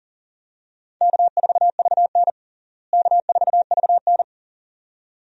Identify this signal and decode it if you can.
Morse code, heard as a keyed tone, about 37 wpm, 700 Hz. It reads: K4VN K4VN
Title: USA Call Signs - 4 Characters - Repeated 2x - 40wpm